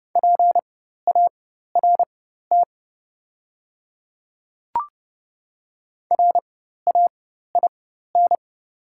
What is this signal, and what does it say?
Morse code, heard as a keyed tone, about 30 wpm, spaced as for 14 wpm, 690 Hz. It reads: PART RAIN
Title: 4x Character Spacing - Top 500 Words - 30wpm